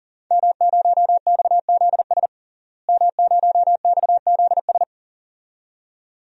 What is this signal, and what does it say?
Morse code, heard as a keyed tone, about 40 wpm, 700 Hz. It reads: M0XZS M0XZS
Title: DXCC-100 Call Signs - 5 Characters - Repeated 2x - 40wpm